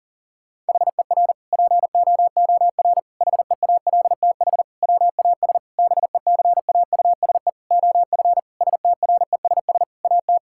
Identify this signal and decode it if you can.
Morse code, heard as a keyed tone, about 37 wpm, 700 Hz. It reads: HER POOR HEALTH WAS BECAUSE OF STRESS AT
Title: Sentences from Top 1000 Words - Review Part 6 - 40wpm